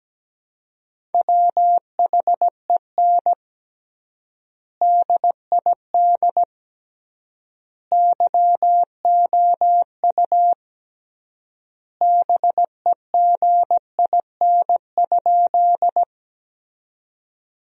Morse code, heard as a keyed tone, about 17 wpm, 700 Hz. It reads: WHEN DID YOU BEGIN?
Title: 3x Word-Spacing - Sentences from Top 300 Words - 17wpm